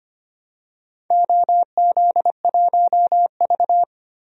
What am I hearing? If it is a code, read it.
Morse code, heard as a keyed tone, about 25 wpm, 700 Hz. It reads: OZ1V
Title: DXCC-100 Call Signs - 4 Characters - 25wpm